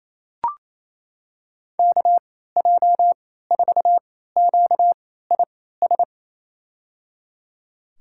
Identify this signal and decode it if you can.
Morse code, heard as a keyed tone, about 28 wpm, spaced as for 16 wpm, 700 Hz. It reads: KJ4QIS
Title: USA Call Signs - 6 Characters - 3x Character Spacing - 28wpm